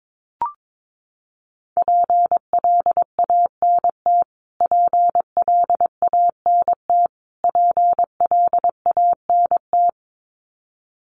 Morse code, heard as a keyed tone, about 22 wpm, 700 Hz. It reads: PLANT PLANT PLANT
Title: Top 500 Words Repeated 22wpm